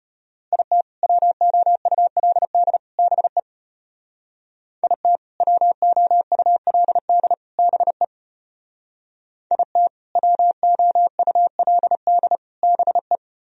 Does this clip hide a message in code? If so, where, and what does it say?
Yes: Morse code from 0.5 to 13.2 s, heard as a keyed tone, about 32 wpm, 700 Hz. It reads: IT WOULD BE IT WOULD BE IT WOULD BE